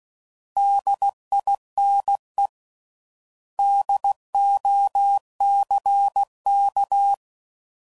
Morse code, heard as a keyed tone, about 16 wpm, 800 Hz. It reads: DINE DOCK
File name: Prefixes / By First Letter / Ds